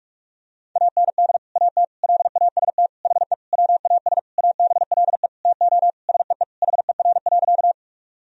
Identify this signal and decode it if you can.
Morse code, heard as a keyed tone, about 47 wpm, 700 Hz. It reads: AND AT LAST HE WAS ABLE TO SEE HER.
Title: Sentences from Top 1000 Words - Review Part 2 - 45wpm